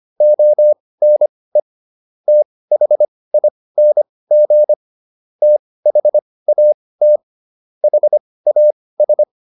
Morse code, heard as a keyed tone, about 25 wpm, spaced as for 18 wpm, 600 Hz. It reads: ONE THING THAT HAS